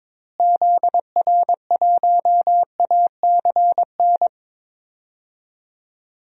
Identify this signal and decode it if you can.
Morse code, heard as a keyed tone, about 22 wpm, 700 Hz. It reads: ZR1ACN
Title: DXCC-100 Call Signs - 6 Characters - Repeated 3x - 22wpm